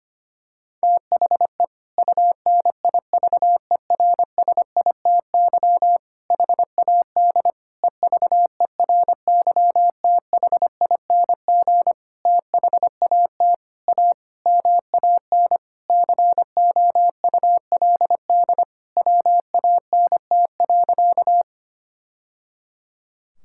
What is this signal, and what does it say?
Morse code, heard as a keyed tone, about 25 wpm, 700 Hz. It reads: THE UNIVERSITY HAD EVERYTHING THAT A MAN COULD WANT.
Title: Sentences from Top 1000 Words - Review Part 5 - 25wpm